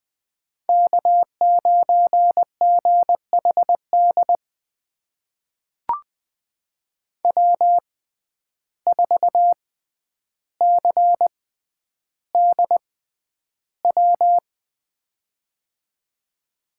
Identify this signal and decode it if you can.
Morse code, heard as a keyed tone, about 20 wpm, spaced as for 10 wpm, 700 Hz. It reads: K9GHD W4CDW
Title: USA Call Signs - 5 Characters - 6x Character Spacing - 20wpm